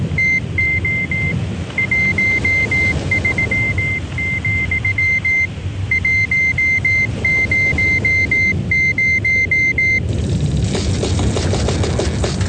Morse code, heard as a keyed tone, about 18 wpm, 2100 Hz. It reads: TO 13, 100